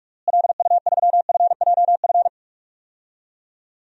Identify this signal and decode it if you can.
Morse code, heard as a keyed tone, about 46 wpm, 700 Hz. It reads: LU3FJF
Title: DXCC-100 Call Signs - 6 Characters - Repeated 2x - 45wpm